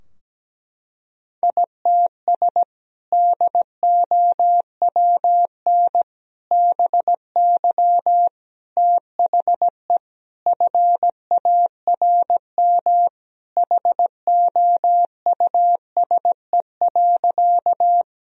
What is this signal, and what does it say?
Morse code, heard as a keyed tone, about 17 wpm, 700 Hz. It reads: ITS DOWN BY THE FARM HOUSE.
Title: Sentences from Top 1000 Words - Review Part 2 - 17wpm